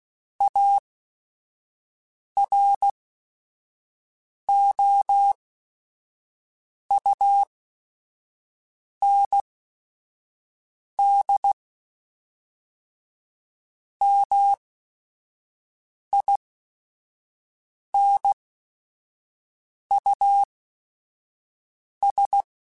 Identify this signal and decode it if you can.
Morse code, heard as a keyed tone, about 16 wpm, spaced as for 5 wpm, 800 Hz. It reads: AROUND MINUS